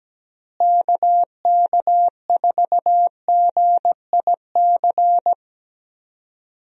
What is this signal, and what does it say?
Morse code, heard as a keyed tone, about 17 wpm, 700 Hz. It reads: KK4GIC
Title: USA Call Signs - 6 Characters - Repeated 2x - 17wpm